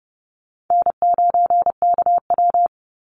Morse code, heard as a keyed tone, about 30 wpm, 700 Hz. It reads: N9KW
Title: NA SOTA Chaser Callsigns 30wpm